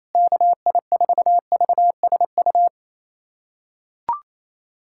Morse code, heard as a keyed tone, about 28 wpm, 700 Hz. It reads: KI4VSU
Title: USA Callsign DE Callsign - 6 Characters - 28wpm